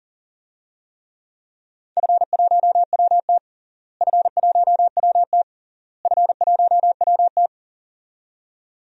Morse code, heard as a keyed tone, about 40 wpm, 700 Hz. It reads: F1WT F1WT F1WT
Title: DXCC-100 Call Signs - 4 Characters - Repeated 3x - 40wpm